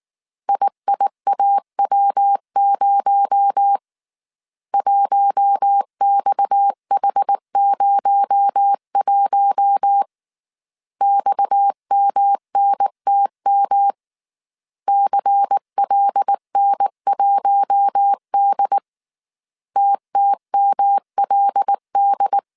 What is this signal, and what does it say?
Morse code, heard as a keyed tone, about 19 wpm, 800 Hz. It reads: IIAW0 1XH01 XMNTM CLN1D TTMLD
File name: CW / DET / 1 6 1500C S A 18wpm 2255